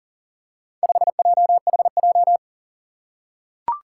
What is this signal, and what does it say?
Morse code, heard as a keyed tone, about 40 wpm, 700 Hz. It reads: 5JHJ
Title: Four Challenging Letter-Number Combinations 40wpm